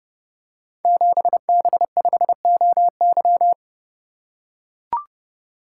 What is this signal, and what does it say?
Morse code, heard as a keyed tone, about 28 wpm, 700 Hz. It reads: 7B5OY
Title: Code Groups - 5 Letter:Numbers - 30wpm